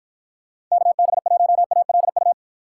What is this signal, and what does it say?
Morse code, heard as a keyed tone, about 49 wpm, 700 Hz. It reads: KB1ALU